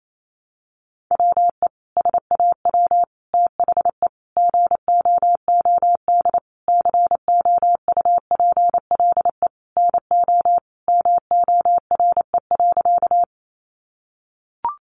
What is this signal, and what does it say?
Morse code, heard as a keyed tone, about 28 wpm, 700 Hz. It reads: WE SAW THE GOOD COUPLE NO MORE.
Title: Sentences from Top 1100 Words 28wpm